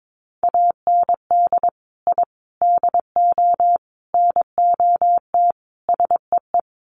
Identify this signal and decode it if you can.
Morse code, heard as a keyed tone, about 22 wpm, 700 Hz. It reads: AND I DO NOT SEE
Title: Sets of 5 Words Encore2 22wpm